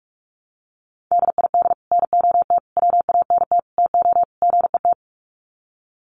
Morse code, heard as a keyed tone, about 44 wpm, 700 Hz. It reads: DID NOT WANT TO GET
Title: Sets of 5 Words Encore2 45wpm